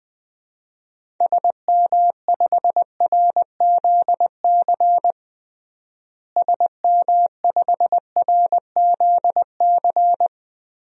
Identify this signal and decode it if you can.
Morse code, heard as a keyed tone, about 20 wpm, 700 Hz. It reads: SM5RZC SM5RZC
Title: DXCC-100 Call Signs - 6 Characters - Repeated 2x - 20wpm